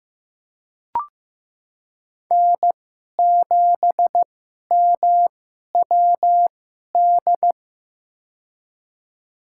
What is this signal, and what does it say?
Morse code, heard as a keyed tone, about 16 wpm, spaced as for 11 wpm, 700 Hz. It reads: N7MWD
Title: USA Call Signs - 5 Characters - 2x Character Spacing - 15wpm